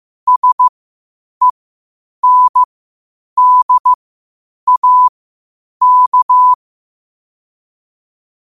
Morse code, heard as a keyed tone, about 15 wpm, spaced as for 8 wpm, 1000 Hz. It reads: SENDAK
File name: qotdfiles / 8wpm